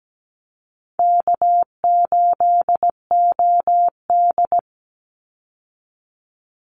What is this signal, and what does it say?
Morse code, heard as a keyed tone, about 17 wpm, 700 Hz. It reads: K8OD
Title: Easy Call Signs - 17wpm